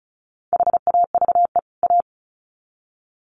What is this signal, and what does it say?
Morse code, heard as a keyed tone, about 34 wpm, 700 Hz. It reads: HAVE A
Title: Sets of 2 Words 35wpm